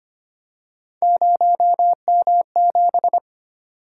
Morse code, heard as a keyed tone, about 25 wpm, 700 Hz. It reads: 0M7
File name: Three Letter-Number Combinations - 25wpm